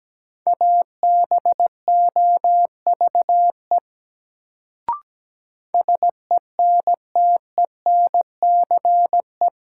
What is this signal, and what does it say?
Morse code, heard as a keyed tone, about 17 wpm, 700 Hz. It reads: ABOVE SENTENCE
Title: Sentences from Top 1200 Words - New Word Drill Alt - 17wpm